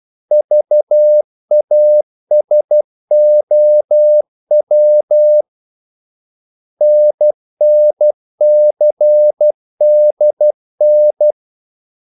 Morse code, heard as a keyed tone, about 12 wpm, 600 Hz. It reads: VASOW NNCDN
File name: code-exemple